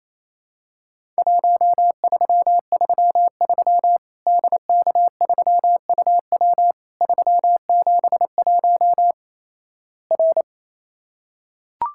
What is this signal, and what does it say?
Morse code, heard as a keyed tone, about 27 wpm, 700 Hz. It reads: 1333 DK3UW 371 R
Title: Worked All Europe DX Contest - QTC Exchange - 28wpm